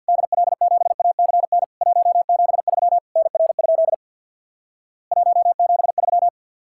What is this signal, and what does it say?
Morse code, heard as a keyed tone, about 50 wpm, 690 Hz. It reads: DL7ACN 173 NR? 173
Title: Worked All Europe DX Contest - QTC Exchange with Repeats - 50wpm